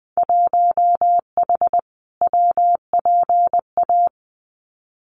Morse code, heard as a keyed tone, about 20 wpm, 700 Hz. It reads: 1H WPA
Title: Winter Field Day Exchanges 20wpm